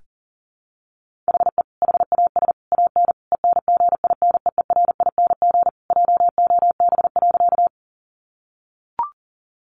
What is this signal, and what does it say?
Morse code, heard as a keyed tone, about 40 wpm, 700 Hz. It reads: HE HAS AN ENGINEERING JOB.
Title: Sentences from Top 1500 Words 40wpm